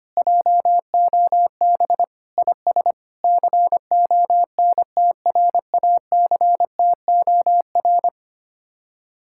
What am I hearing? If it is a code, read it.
Morse code, heard as a keyed tone, about 24 wpm, 700 Hz. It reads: JOB IS CONTRACTOR
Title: Speed-Racing Common QSO Phrases - 25wpm